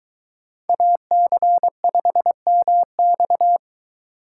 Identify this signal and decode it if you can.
Morse code, heard as a keyed tone, about 23 wpm, 700 Hz. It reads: AC5MX